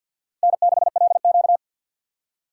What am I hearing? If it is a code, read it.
Morse code, heard as a keyed tone, about 47 wpm, 700 Hz. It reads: N6LQ
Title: USA Call Signs - 6x Character Spacing - 50wpm